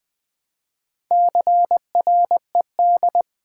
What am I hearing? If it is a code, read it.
Morse code, heard as a keyed tone, about 20 wpm, 700 Hz. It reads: CRED
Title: Build A Word - 2 Letters At A Time - 6-Letter Words - 20wpm